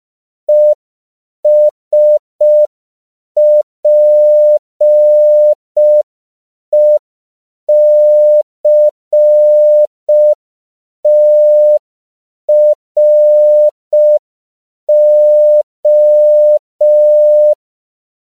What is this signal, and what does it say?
Morse code, heard as a keyed tone, about 5 wpm, 600 Hz. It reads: ESPECTRO
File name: CW / 5PPM